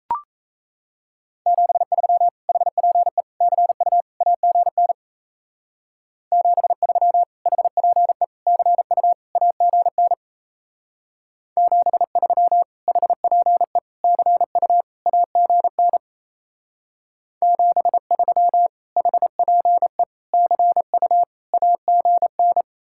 Morse code, heard as a keyed tone, about 36 wpm, 700 Hz. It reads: 73 HPE CU AGN 73 HPE CU AGN 73 HPE CU AGN 73 HPE CU AGN